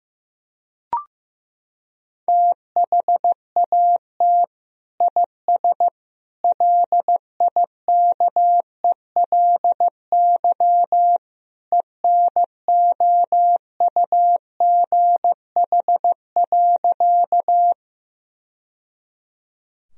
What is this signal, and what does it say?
Morse code, heard as a keyed tone, about 15 wpm, 700 Hz. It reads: THAT IS LIKELY ENOUGH.
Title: Sentences from Top 1000 Words - Review Part 4 - 15wpm